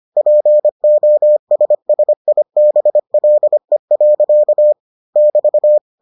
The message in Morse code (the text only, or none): POSSIBLE. <BT>